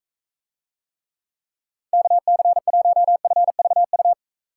Morse code, heard as a keyed tone, about 44 wpm, 700 Hz. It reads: KC1FVU